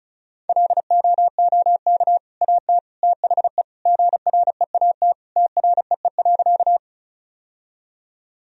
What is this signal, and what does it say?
Morse code, heard as a keyed tone, about 36 wpm, 700 Hz. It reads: LOOK AT THE GREAT TREE.
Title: Sentences from Top 1000 Words - Review Part 1 - 35wpm